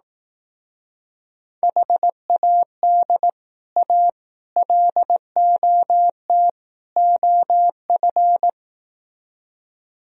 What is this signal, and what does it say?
Morse code, heard as a keyed tone, about 18 wpm, 700 Hz. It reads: HAD A LOT OF
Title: Speed-Racing - Sets of 4 Words 15wpm